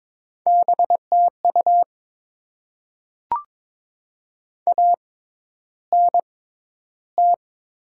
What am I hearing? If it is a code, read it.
Morse code, heard as a keyed tone, about 22 wpm, spaced as for 11 wpm, 700 Hz. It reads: BTU ANT
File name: Top 20 QSO Elements - 6x Character Spacing - 22wpm